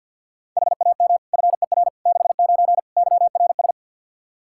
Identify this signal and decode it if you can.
Morse code, heard as a keyed tone, about 48 wpm, 700 Hz. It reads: HAM FER 69 YRS
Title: K1USN SST QSO Practice 50wpm